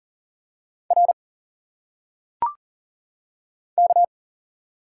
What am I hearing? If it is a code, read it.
Morse code, heard as a keyed tone, about 40 wpm, 700 Hz. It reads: R K
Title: Single Letters - Phonetic Pronunciation 40wpm